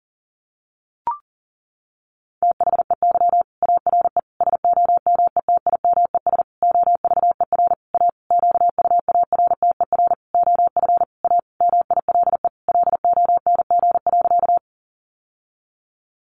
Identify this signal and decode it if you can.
Morse code, heard as a keyed tone, about 40 wpm, 700 Hz. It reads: THEY ARE SOMETIMES OVER A QUARTER OF A MILE LONG.
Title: Sentences from Top 1300 Words 40wpm